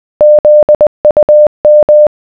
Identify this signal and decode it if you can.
Morse code, heard as a keyed tone, about 20 wpm, 600 Hz. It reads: ZUM